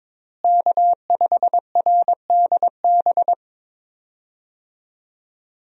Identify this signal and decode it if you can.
Morse code, heard as a keyed tone, about 22 wpm, 700 Hz. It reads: K5RDB
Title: USA Call Signs - 5 Characters - Repeated 3x - 22wpm